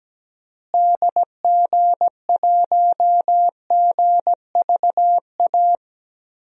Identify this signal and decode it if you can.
Morse code, heard as a keyed tone, about 17 wpm, 700 Hz. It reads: DG1GVA
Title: DXCC-100 Call Signs - 6 Characters - Repeated 2x - 17wpm